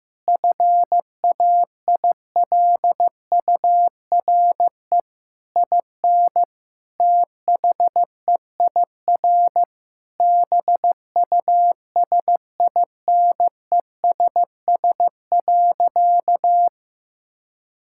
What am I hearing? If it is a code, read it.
Morse code, heard as a keyed tone, about 15 wpm, 700 Hz. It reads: FAILURE IN THEIR BUSINESS.
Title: Sentences from Top 1000 Words - Review Part 6 - 15wpm